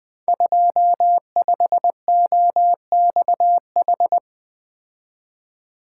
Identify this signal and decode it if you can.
Morse code, heard as a keyed tone, about 20 wpm, 700 Hz. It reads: 25OXH
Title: Code Groups - 5 Letter:Numbers - 20wpm